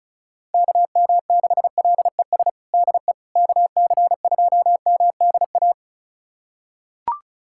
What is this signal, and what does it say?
Morse code, heard as a keyed tone, about 34 wpm, 700 Hz. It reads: KM6LES DE KC2MDA
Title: USA Callsign DE Callsign - 6 Characters - 35wpm